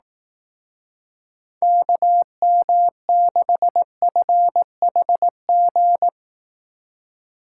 Morse code, heard as a keyed tone, about 18 wpm, 700 Hz. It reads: KM6FHG
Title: Speed Racing - USA 2x3 Call Signs 15wpm